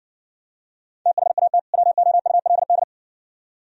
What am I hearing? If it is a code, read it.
Morse code, heard as a keyed tone, about 60 wpm, 700 Hz. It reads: THAT WOULD